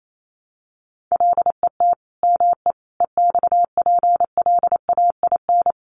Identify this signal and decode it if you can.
Morse code, heard as a keyed tone, about 28 wpm, 700 Hz. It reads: LET ME EXPLAIN
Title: Sentences from Top 1300 Words 28wpm